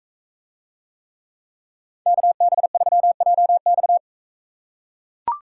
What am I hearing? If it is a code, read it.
Morse code, heard as a keyed tone, about 43 wpm, 700 Hz. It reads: KB3JX